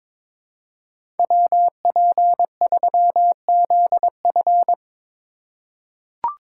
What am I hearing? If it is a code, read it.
Morse code, heard as a keyed tone, about 22 wpm, 700 Hz. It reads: WP3ZF
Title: USA Call Signs - 5 Characters - 4x Character Spacing - 22wpm